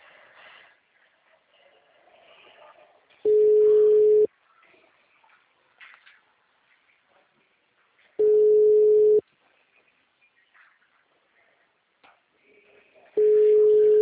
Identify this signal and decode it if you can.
Ringback tone (Europe, 425 Hz).
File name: out- -2016- - - .28077